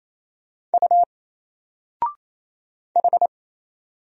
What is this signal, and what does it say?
Morse code, heard as a keyed tone, about 27 wpm, 700 Hz. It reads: U H
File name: Single-Letter-Number-28wpm